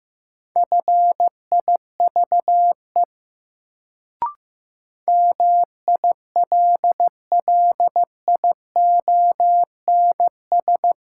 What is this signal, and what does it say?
Morse code, heard as a keyed tone, about 15 wpm, 700 Hz. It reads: FIVE MILLIONS
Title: Sets of 5 Words - New Word Drill - 15wpm